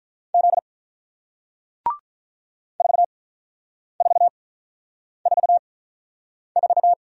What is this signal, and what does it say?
Morse code, heard as a keyed tone, about 48 wpm, spaced as for 9 wpm, 700 Hz. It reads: Z 4444